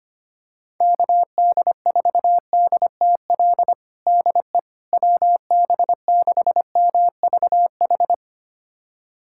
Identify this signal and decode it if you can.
Morse code, heard as a keyed tone, about 25 wpm, 700 Hz. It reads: KD4DTL DE WB6MVH
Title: USA Callsign DE Callsign - 6 Characters - 25wpm